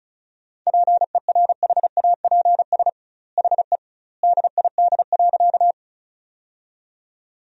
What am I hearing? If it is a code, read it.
Morse code, heard as a keyed tone, about 34 wpm, 700 Hz. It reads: PERHAPS HE DID.
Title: 2x Word-Spacing - Sentences from Top 500 Words - 35wpm